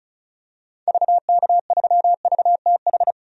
Morse code, heard as a keyed tone, about 34 wpm, 700 Hz. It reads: VK3VTH
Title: DXCC-100 Call Signs - 6 Characters - 35wpm